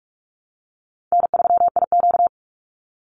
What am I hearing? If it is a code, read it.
Morse code, heard as a keyed tone, about 44 wpm, 700 Hz. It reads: N3IQ
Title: Easy Call Signs Repeated 2x - 45wpm